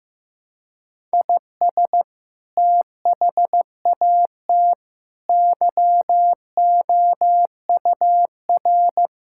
Morse code, heard as a keyed tone, about 15 wpm, 700 Hz. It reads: IS THAT YOUR
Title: Sentences from Top 1000 Words - Review Part 5 - 15wpm